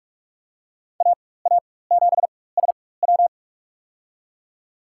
Morse code, heard as a keyed tone, about 44 wpm, spaced as for 21 wpm, 690 Hz. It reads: AA7SW
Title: USA Call Signs - 4x Character Spacing - 45wpm